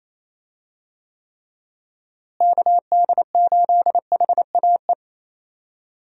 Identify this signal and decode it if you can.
Morse code, heard as a keyed tone, about 27 wpm, 700 Hz. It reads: KD8HAE
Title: USA Call Signs - 6 Characters - Repeated 3x - 28wpm